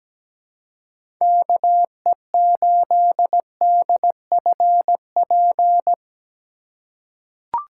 Morse code, heard as a keyed tone, about 17 wpm, 700 Hz. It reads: KE8DFP
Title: USA Call Signs - 6 Characters - 2x Character Spacing - 17wpm